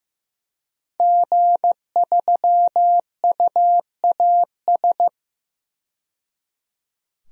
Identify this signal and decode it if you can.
Morse code, heard as a keyed tone, about 15 wpm, 700 Hz. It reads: G3UAS
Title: DXCC-100 Call Signs - 5 Characters - 15wpm